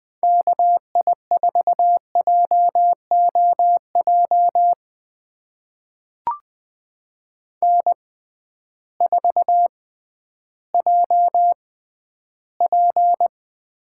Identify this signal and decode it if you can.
Morse code, heard as a keyed tone, about 20 wpm, spaced as for 12 wpm, 700 Hz. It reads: KI4JOJ N4JP
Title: USA Call Signs - 6x Character Spacing - 20wpm